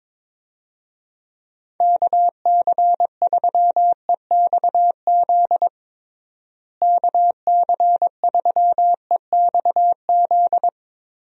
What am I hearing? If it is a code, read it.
Morse code, heard as a keyed tone, about 22 wpm, 700 Hz. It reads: KC3EXZ KC3EXZ
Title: USA Call Signs - 6 Characters - Repeated 3x - 22wpm